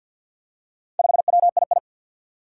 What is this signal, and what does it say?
Morse code, heard as a keyed tone, about 52 wpm, 700 Hz. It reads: 5WII
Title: Four Challenging Letter-Number Combinations 50wpm